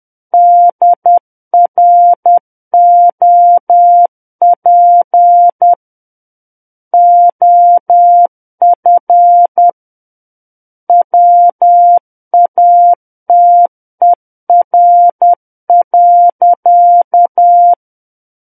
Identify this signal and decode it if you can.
Morse code, heard as a keyed tone, about 10 wpm, 700 Hz. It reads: DROP OF WATER.
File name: Book One / 01-Chapter 1 - 10 WPM